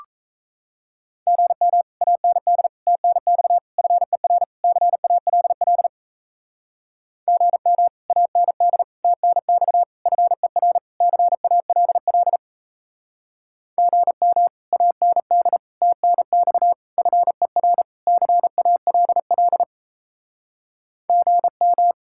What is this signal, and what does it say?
Morse code, heard as a keyed tone, about 34 wpm, 700 Hz. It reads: GM AND TNX FER CALL GM AND TNX FER CALL GM AND TNX FER CALL GM
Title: Speed-Racing Common QSO Phrases Encore - 28wpm